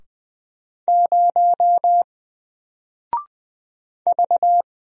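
Morse code, heard as a keyed tone, about 20 wpm, 700 Hz. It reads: 0 V
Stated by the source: Single letter and number Morse drill at 20 wpm, from N4LCD.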